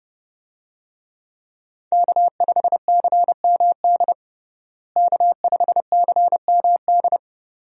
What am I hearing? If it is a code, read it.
Morse code, heard as a keyed tone, about 31 wpm, 700 Hz. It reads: K5CMD K5CMD
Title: USA Call Signs - 5 Characters - Repeated 2x - 30wpm